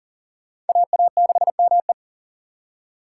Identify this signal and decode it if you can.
Morse code, heard as a keyed tone, about 52 wpm, 700 Hz. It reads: AA6ME